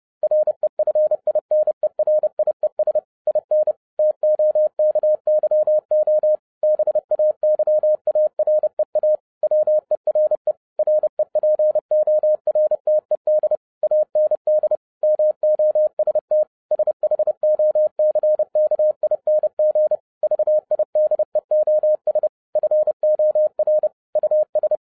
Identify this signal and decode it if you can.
Morse code, heard as a keyed tone, about 30 wpm, 610 Hz. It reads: REFINERIES IN TOKYO BAY AREA WERE REPORTED AND MOST SHOCKING VIDEOS FOR US